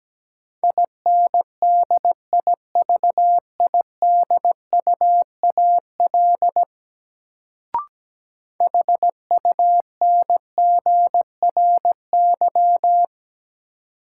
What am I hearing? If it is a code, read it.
Morse code, heard as a keyed tone, about 17 wpm, 700 Hz. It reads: INDIVIDUAL HUNGRY
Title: Sentences from Top 1400 Words - New Word Drill Alt - 17wpm